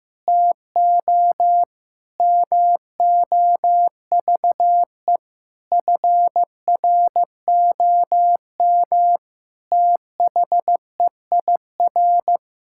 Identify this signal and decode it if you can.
Morse code, heard as a keyed tone, about 15 wpm, 700 Hz. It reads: TO MOVE FROM THEIR
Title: Sentences from Top 1000 Words - Review Part 6 - 15wpm